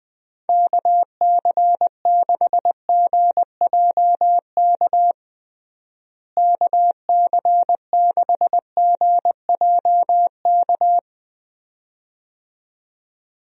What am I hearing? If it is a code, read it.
Morse code, heard as a keyed tone, about 20 wpm, 700 Hz. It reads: KC6GJK KC6GJK
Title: USA Call Signs - 6 Characters - Repeated 2x - 20wpm